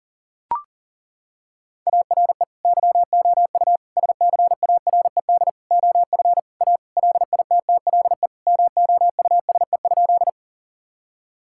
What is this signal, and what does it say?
Morse code, heard as a keyed tone, about 37 wpm, 700 Hz. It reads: ARE YOU SCARED OF A LITTLE MOUSE?
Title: Sentences from Top 1000 Words - Review Part 5 - 40wpm